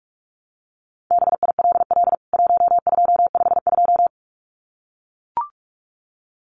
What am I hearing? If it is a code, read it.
Morse code, heard as a keyed tone, about 44 wpm, 700 Hz. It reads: BILL 1252
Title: CWT Exchange 45wpm